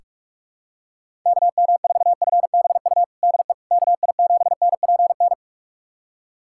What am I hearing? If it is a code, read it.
Morse code, heard as a keyed tone, about 44 wpm, 700 Hz. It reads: KM4FBU DE KI7NPN
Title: USA Callsign DE Callsign 45wpm